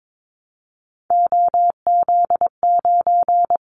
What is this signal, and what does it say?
Morse code, heard as a keyed tone, about 22 wpm, 700 Hz. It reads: OZ9
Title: Callsign Prefixes for Top 50 DXCC - 22wpm